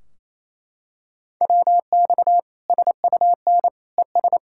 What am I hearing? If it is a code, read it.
Morse code, heard as a keyed tone, about 28 wpm, 700 Hz. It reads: WX SUN ES
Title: Practice QSO - Exchange 2 - Info - 28wpm